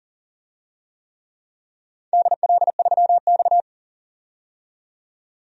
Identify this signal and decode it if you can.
Morse code, heard as a keyed tone, about 40 wpm, 700 Hz. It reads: DL3X
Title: DXCC-100 Call Signs - 4 Characters - 40wpm